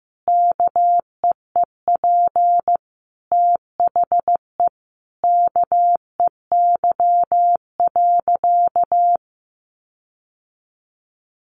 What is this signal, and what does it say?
Morse code, heard as a keyed tone, about 15 wpm, 700 Hz. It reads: KEEP THE KEY.
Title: Sentences from Top 1300 Words 15wpm